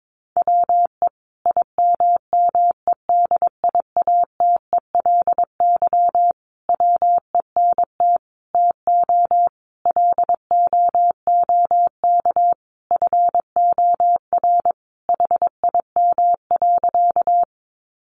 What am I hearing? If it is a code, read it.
Morse code, heard as a keyed tone, about 22 wpm, 700 Hz. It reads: WE IMMEDIATELY WENT TO LOOK FOR HIM.
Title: Sentences from Top 1100 Words 22wpm